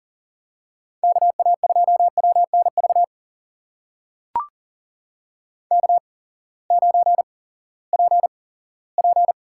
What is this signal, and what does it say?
Morse code, heard as a keyed tone, about 40 wpm, spaced as for 20 wpm, 700 Hz. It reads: KA2WNV K9PP
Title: USA Call Signs - 8x Character Spacing - 40wpm